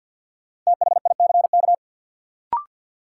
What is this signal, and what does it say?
Morse code, heard as a keyed tone, about 50 wpm, 700 Hz. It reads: THICK
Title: Sentences from Top 1600 Words - New Word Drill - 50wpm